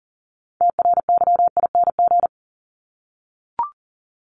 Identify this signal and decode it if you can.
Morse code, heard as a keyed tone, about 40 wpm, 700 Hz. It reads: TRYING
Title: ING Suffix - Top 100 Words with Base Words 40wpm